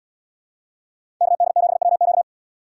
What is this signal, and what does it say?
Morse code, heard as a keyed tone, about 72 wpm, 700 Hz. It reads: KD7WQ